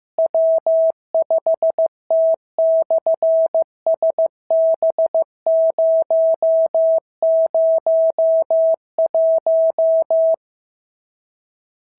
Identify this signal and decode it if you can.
Morse code, heard as a keyed tone, about 15 wpm, 650 Hz. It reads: W5T/SB001
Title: SOTA QSO Practice 15wpm